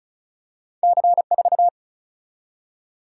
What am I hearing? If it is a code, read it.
Morse code, heard as a keyed tone, about 36 wpm, 700 Hz. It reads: C 4
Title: Single-Letter Single-Number 35wpm